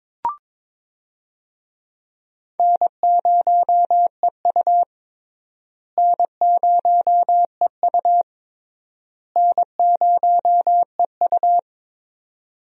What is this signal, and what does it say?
Morse code, heard as a keyed tone, about 22 wpm, 700 Hz. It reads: N0EU N0EU N0EU
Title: USA Call Signs - 4 Characters - Repeated 3x - 22wpm